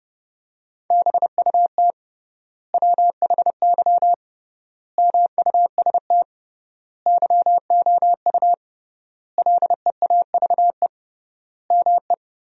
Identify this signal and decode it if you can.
Morse code, heard as a keyed tone, about 30 wpm, 700 Hz. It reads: BUT WHY MUST YOU LEAVE ME
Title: 3x Word-Spacing - Sentences from Top 400 Words - 30wpm